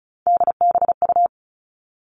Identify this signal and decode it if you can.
Morse code, heard as a keyed tone, about 34 wpm, 700 Hz. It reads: DBU
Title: All Cut Numbers - 3-Digit 35wpm